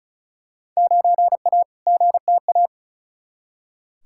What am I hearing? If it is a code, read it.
Morse code, heard as a keyed tone, about 34 wpm, 700 Hz. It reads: 9A GTA
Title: Field Day Exchanges Encore - 35wpm